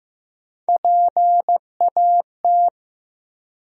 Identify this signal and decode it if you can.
Morse code, heard as a keyed tone, about 15 wpm, 700 Hz. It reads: PAT
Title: USA Common Names - Max 3 Letters - 15wpm